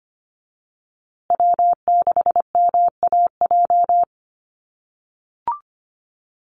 Morse code, heard as a keyed tone, about 25 wpm, 700 Hz. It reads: W6MAJ
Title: USA Call Signs Repeated 3x 25wpm